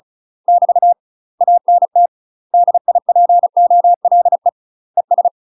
Morse code, heard as a keyed tone, about 36 wpm, 700 Hz. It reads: <BT> ANT DIPOLE ES
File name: UK MASTER QSOs TRAINING / 35WPM / UK MASTER QSOs 0041